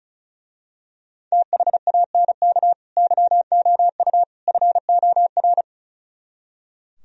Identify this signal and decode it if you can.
Morse code, heard as a keyed tone, about 32 wpm, 700 Hz. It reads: THANK YOU FOR
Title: Speed-Racing - Sets of 3 Words 35wpm